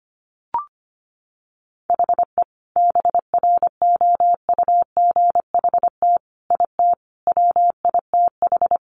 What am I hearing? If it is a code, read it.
Morse code, heard as a keyed tone, about 25 wpm, 700 Hz. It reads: HE BROUGHT IT WITH
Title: Sentences from Top 500 Words 25wpm